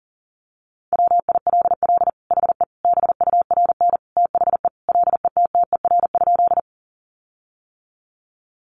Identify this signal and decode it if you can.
Morse code, heard as a keyed tone, about 40 wpm, 700 Hz. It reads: WILL HE BURN THE LETTER?
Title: Sentences from Top 1300 Words 40wpm